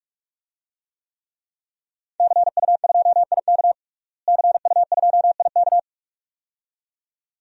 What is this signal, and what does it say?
Morse code, heard as a keyed tone, about 44 wpm, 700 Hz. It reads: CU2IK CU2IK
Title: DXCC-100 Call Signs - 5 Characters - Repeated 2x - 45wpm